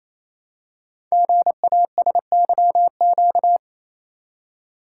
Morse code, heard as a keyed tone, about 28 wpm, 700 Hz. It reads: GASYQ
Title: Code Groups - 5 Letter:Numbers - 28wpm